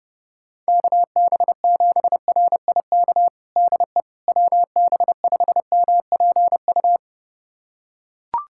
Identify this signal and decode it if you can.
Morse code, heard as a keyed tone, about 30 wpm, 700 Hz. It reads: KB7RIK DE WB5MPU
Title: USA Callsign DE Callsign - 6 Characters - 30wpm